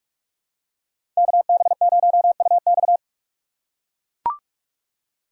Morse code, heard as a keyed tone, about 44 wpm, 700 Hz. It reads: KB0UX
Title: USA Call Signs - 5 Characters - 6x Character Spacing - 45wpm